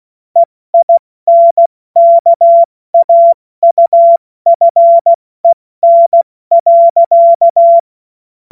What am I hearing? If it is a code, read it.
Morse code, heard as a keyed tone, about 16 wpm, 680 Hz. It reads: EINKAUFEN.
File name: Morsekurs / Witze / Witz 29. Juni 2021, 70 F80